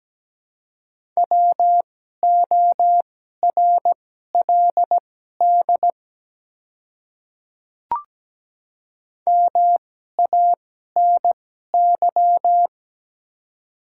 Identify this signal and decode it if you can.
Morse code, heard as a keyed tone, about 17 wpm, spaced as for 12 wpm, 700 Hz. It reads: WORLD MANY
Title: 2x Character Spacing - Top 200 Words - 17wpm